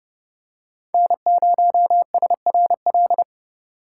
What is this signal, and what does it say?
Morse code, heard as a keyed tone, about 30 wpm, 700 Hz. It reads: N0SRL
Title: USA Call Signs - 5 Characters - 2x Character Spacing - 30wpm